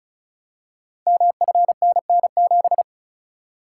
Morse code, heard as a keyed tone, about 36 wpm, 700 Hz. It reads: MFNN7
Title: Code Groups - 5 Letter:Numbers - 35wpm